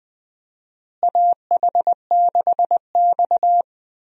Morse code, heard as a keyed tone, about 21 wpm, 700 Hz. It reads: AH6X
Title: USA Call Signs - 4 Characters - Repeated 2x - 20wpm